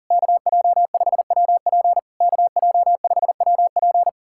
Morse code, heard as a keyed tone, about 44 wpm, 700 Hz. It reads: KJ5WP KJ5WP
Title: SOTA QSO Practice 40wpm